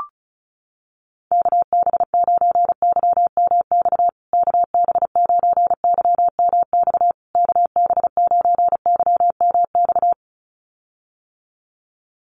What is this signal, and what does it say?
Morse code, heard as a keyed tone, about 34 wpm, 700 Hz. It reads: KB9YMX KB9YMX KB9YMX
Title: USA Call Signs Repeated 3x 35wpm